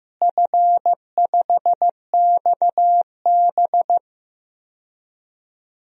Morse code, heard as a keyed tone, about 15 wpm, 700 Hz. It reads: F5XB
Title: DXCC-100 Call Signs - 4 Characters - Repeated 3x - 15wpm